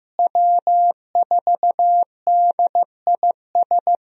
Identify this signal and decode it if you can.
Morse code, heard as a keyed tone, about 16 wpm, 700 Hz. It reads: W4DIS
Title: USA Call Signs - 5 Characters - Repeated 3x - 15wpm